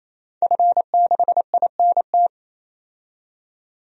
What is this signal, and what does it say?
Morse code, heard as a keyed tone, about 28 wpm, 700 Hz. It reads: F6INT
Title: DXCC-100 Call Signs - 5 Characters - 28wpm